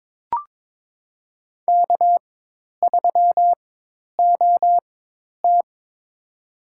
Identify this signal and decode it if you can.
Morse code, heard as a keyed tone, about 22 wpm, spaced as for 10 wpm, 700 Hz. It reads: K3OT
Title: USA Call Signs - 4 Characters - 4x Character Spacing - 22wpm